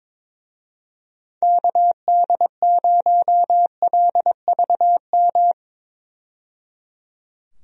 Morse code, heard as a keyed tone, about 22 wpm, 700 Hz. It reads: KD0LVM
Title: USA Call Signs - 6 Characters - 22wpm